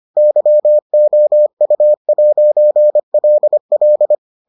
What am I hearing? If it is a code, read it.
Morse code, heard as a keyed tone, about 25 wpm, 600 Hz. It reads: YOU'LL